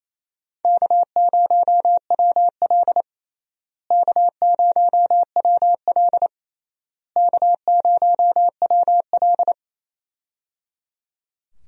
Morse code, heard as a keyed tone, about 28 wpm, 700 Hz. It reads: K0WL K0WL K0WL
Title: USA Call Signs - 4 Characters - Repeated 3x - 28wpm